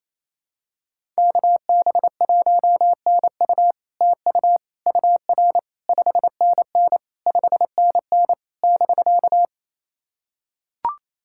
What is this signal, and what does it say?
Morse code, heard as a keyed tone, about 27 wpm, 700 Hz. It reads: KB1NU TU UR 5NN 5NN <BK>
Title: SOTA QSO Practice 28wpm